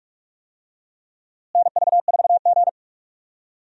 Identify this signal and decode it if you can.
Morse code, heard as a keyed tone, about 46 wpm, 700 Hz. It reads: NV4G
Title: USA Call Signs - 4 Characters - Repeated 3x - 45wpm